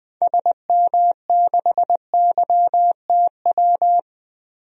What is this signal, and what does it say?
Morse code, heard as a keyed tone, about 21 wpm, 700 Hz. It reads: SM6YTW
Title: DXCC-100 Call Signs - 6 Characters - Repeated 2x - 20wpm